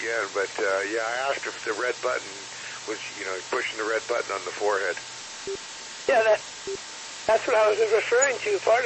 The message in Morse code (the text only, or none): none